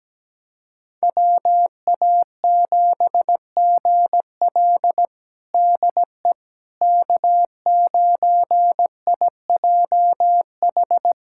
Morse code, heard as a keyed tone, about 17 wpm, 700 Hz. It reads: WA7GL DE K9IJH
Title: USA Callsign DE Callsign - 5 Characters - 17wpm